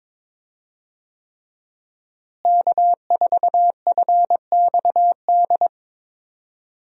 Morse code, heard as a keyed tone, about 21 wpm, 700 Hz. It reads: K4FXD
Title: USA Call Signs - 5 Characters - Repeated 3x - 22wpm